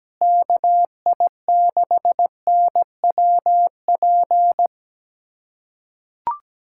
Morse code, heard as a keyed tone, about 17 wpm, 700 Hz. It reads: KI6NWP
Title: USA Callsign DE Callsign - 6 Characters - 17wpm